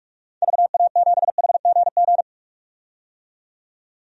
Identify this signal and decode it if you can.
Morse code, heard as a keyed tone, about 46 wpm, 700 Hz. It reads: VA7HGG
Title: DXCC-100 Call Signs - 6 Characters - 45wpm